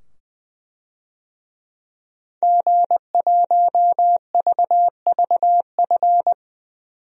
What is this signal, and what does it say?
Morse code, heard as a keyed tone, about 20 wpm, 700 Hz. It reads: G1VVF
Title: DXCC-100 Call Signs - 5 Characters - 20wpm